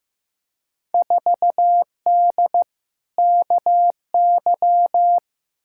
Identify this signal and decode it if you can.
Morse code, heard as a keyed tone, about 15 wpm, 700 Hz. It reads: 4D KY
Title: Field Day Exchanges Encore - 15wpm